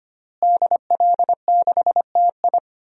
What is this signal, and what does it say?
Morse code, heard as a keyed tone, about 25 wpm, 700 Hz. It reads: DL6TI
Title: DXCC-100 Call Signs - 5 Characters - Repeated 2x - 25wpm